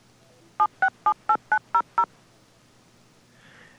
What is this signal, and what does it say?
DTMF, heard as telephone touch tones, tones of about 90 ms. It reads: 7678900
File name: acc / acc-rc-850-and-rc-85-demo